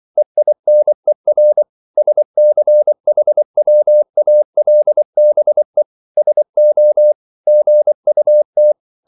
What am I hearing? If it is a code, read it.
Morse code, heard as a keyed tone, about 24 wpm, 600 Hz. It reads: EINER SCHWALBE SO GUT